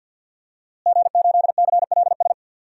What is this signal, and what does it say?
Morse code, heard as a keyed tone, about 51 wpm, 700 Hz. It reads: G8CLS
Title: DXCC-100 Call Signs - 5 Characters - Repeated 3x - 50wpm